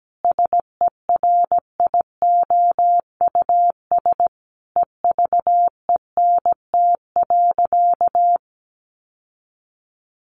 Morse code, heard as a keyed tone, about 17 wpm, 700 Hz. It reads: SERIOUS EVENT.
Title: Sentences from Top 1400 Words 17wpm